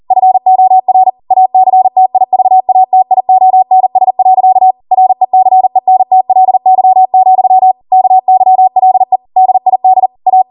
Morse code, heard as a keyed tone, about 40 wpm, 750 Hz. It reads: FOR ACTIVATIONS. RECENTLY, KYLE DID A